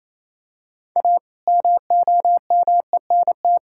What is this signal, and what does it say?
Morse code, heard as a keyed tone, about 28 wpm, 700 Hz. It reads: A MOMENT
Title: Sentences from Top 1000 Words - Review Part 6 - 28wpm